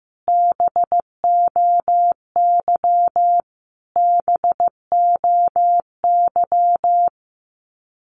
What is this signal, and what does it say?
Morse code, heard as a keyed tone, about 15 wpm, 700 Hz. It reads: BOY BOY
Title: Top 200 Words Repeated 15wpm